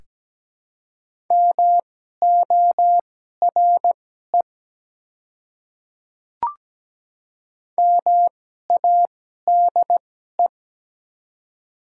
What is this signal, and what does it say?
Morse code, heard as a keyed tone, about 17 wpm, spaced as for 12 wpm, 700 Hz. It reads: MORE MADE
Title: 2x Character Spacing - Top 200 Words - 17wpm